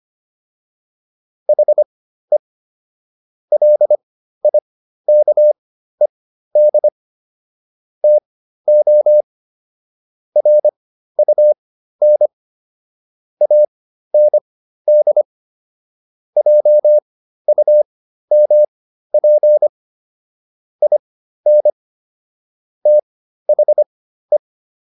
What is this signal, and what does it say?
Morse code, heard as a keyed tone, about 25 wpm, spaced as for 13 wpm, 600 Hz. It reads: HE LIKED TO RUN AND JUMP IN THE